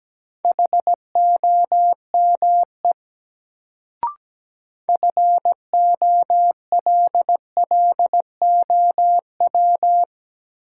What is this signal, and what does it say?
Morse code, heard as a keyed tone, about 17 wpm, 700 Hz. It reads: HOME FOLLOW
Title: Sentences from Top 200 Words - New Word Drill - 17wpm